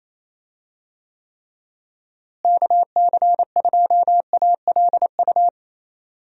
Morse code, heard as a keyed tone, about 28 wpm, 700 Hz. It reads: KC2ALU
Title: USA Call Signs - 6 Characters - Repeated 2x - 28wpm